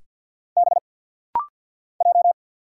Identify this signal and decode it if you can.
Morse code, heard as a keyed tone, about 50 wpm, 700 Hz. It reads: B J
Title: Single Letter-Number - Rapid-Fire 50wpm